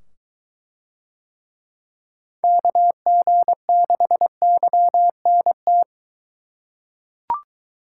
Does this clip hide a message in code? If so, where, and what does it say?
Yes: Morse code from 2.4 to 5.8 s, heard as a keyed tone, about 22 wpm, 700 Hz. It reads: KG6YNT